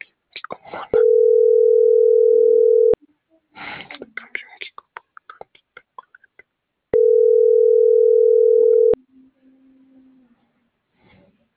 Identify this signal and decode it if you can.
Ringback tone (North America), 440 + 480 Hz, with speech.
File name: CENTRAL NAVIERAS / out onelinecapacitacion - .30304